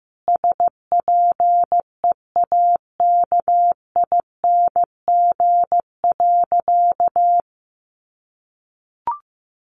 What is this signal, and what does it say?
Morse code, heard as a keyed tone, about 15 wpm, 700 Hz. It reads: SPEAKING.
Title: A Sound Of Thunder 15wpm